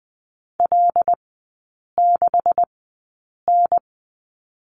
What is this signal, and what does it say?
Morse code, heard as a keyed tone, about 20 wpm, spaced as for 8 wpm, 700 Hz. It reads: L6N
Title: 127 Lesson 21e - Number 6 - Three Characters 20wpm